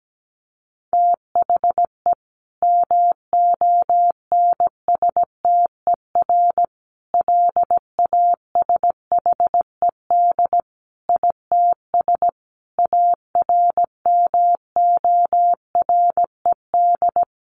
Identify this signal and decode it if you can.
Morse code, heard as a keyed tone, about 17 wpm, 700 Hz. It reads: THE MONSTER LASHED ITS ARMORED
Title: A Sound Of Thunder 17wpm